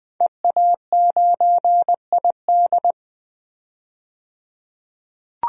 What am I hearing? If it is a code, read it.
Morse code, heard as a keyed tone, about 20 wpm, 700 Hz. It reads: EA9ID
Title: DXCC-100 Call Signs - 5 Characters - 20wpm